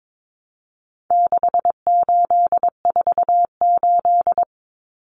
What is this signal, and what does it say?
Morse code, heard as a keyed tone, about 22 wpm, 700 Hz. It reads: 6848
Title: Numbers - 4-Digit 22wpm